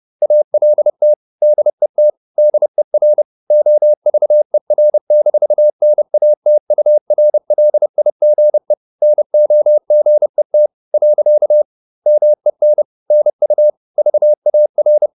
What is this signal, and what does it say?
Morse code, heard as a keyed tone, about 30 wpm, 600 Hz. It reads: ALT DET DER OVER-NATURLIGE NOGET . MEN NU VAR
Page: Ferie 30 wpm